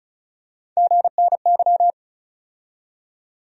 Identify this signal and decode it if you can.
Morse code, heard as a keyed tone, about 35 wpm, 700 Hz. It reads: GNY
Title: USA Call Sign Postfixes - 3 Letters 35wpm